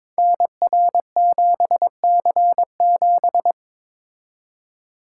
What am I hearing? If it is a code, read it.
Morse code, heard as a keyed tone, about 22 wpm, 700 Hz. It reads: NR7C7
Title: Code Groups - 5 Letter:Numbers - 22wpm